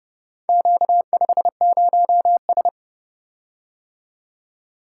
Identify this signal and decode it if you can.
Morse code, heard as a keyed tone, about 29 wpm, 700 Hz. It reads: Q50S